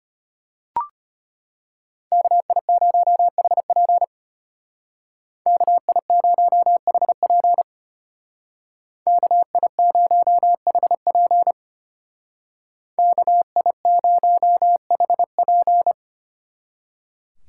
Morse code, heard as a keyed tone, about 30 wpm, 700 Hz. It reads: KI0HP KI0HP KI0HP KI0HP
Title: Speed Racing - USA 2x2 Call Signs 25wpm